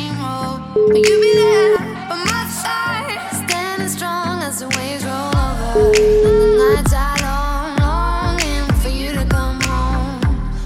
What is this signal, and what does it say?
Ringback tone (Europe, 425 Hz), with noise.